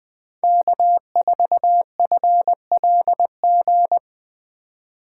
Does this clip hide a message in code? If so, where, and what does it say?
Yes: Morse code from 0.4 to 4.0 s, heard as a keyed tone, about 21 wpm, 700 Hz. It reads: K4FLG